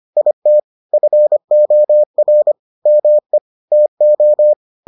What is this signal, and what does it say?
Morse code, heard as a keyed tone, about 25 wpm, 600 Hz. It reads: IT FOR ME TO